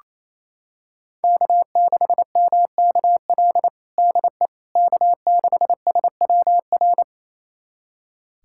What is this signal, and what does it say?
Morse code, heard as a keyed tone, about 28 wpm, 700 Hz. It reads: K6MKL DE K6SWR
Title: USA Callsign DE Callsign - 5 Characters - 28wpm